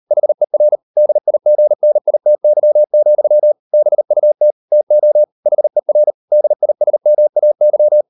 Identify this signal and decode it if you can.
Morse code, heard as a keyed tone, about 38 wpm, 600 Hz. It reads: HER DIGNITY, BUT TO HER DISMAY